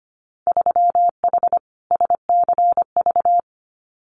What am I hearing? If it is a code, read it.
Morse code, heard as a keyed tone, about 25 wpm, 700 Hz. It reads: 3H SCV
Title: Winter Field Day Exchanges Encore 25wpm